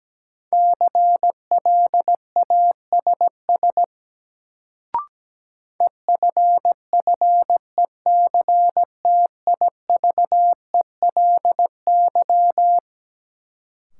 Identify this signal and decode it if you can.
Morse code, heard as a keyed tone, about 17 wpm, 700 Hz. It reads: CLASS EFFECTIVELY
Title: Sentences from Top 1400 Words - New Word Drill Alt - 17wpm